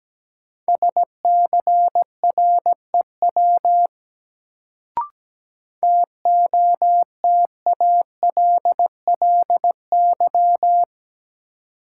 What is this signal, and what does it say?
Morse code, heard as a keyed tone, about 17 wpm, 700 Hz. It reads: SCREW TOTALLY
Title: Sentences from Top 1600 Words - New Word Drill - 17wpm